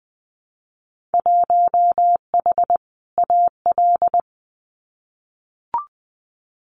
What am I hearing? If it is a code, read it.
Morse code, heard as a keyed tone, about 20 wpm, 700 Hz. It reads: 1H AL
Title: Winter Field Day Exchanges 20wpm